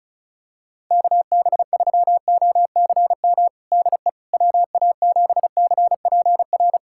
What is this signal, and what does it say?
Morse code, heard as a keyed tone, about 34 wpm, 700 Hz. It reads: KB3OCM DE WA7CPR
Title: USA Callsign DE Callsign 35wpm